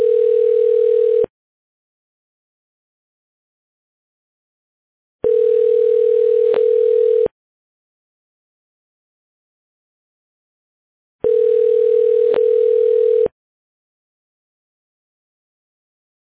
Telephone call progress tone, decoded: ringback tone